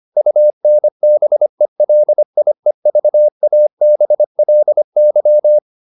Morse code, heard as a keyed tone, about 25 wpm, 600 Hz. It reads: UNBELIEVABLY
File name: the old man and the sea part 1 25wpm